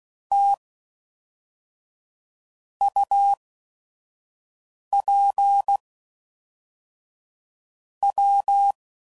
Morse code, heard as a keyed tone, about 16 wpm, spaced as for 5 wpm, 800 Hz. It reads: T UP W
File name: CW Code Groups @5WPM / QSOs / QSO WD0FBT